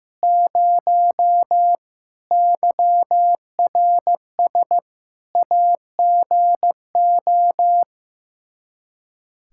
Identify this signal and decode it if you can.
Morse code, heard as a keyed tone, about 15 wpm, 700 Hz. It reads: 0 YRS AGO
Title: Practice QSO - Exchange 3 - More Info - 15wpm